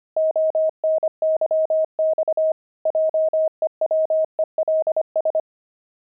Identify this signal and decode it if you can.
Morse code, heard as a keyed tone, about 25 wpm, 630 Hz. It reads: ONYX JEWELS
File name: pangram-fs-25-cs-25-grandpa